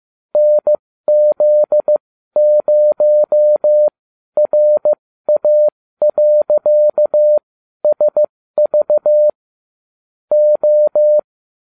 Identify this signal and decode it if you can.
Morse code, heard as a keyed tone, about 15 wpm, spaced as for 12 wpm, 600 Hz. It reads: NZ0RA.SV O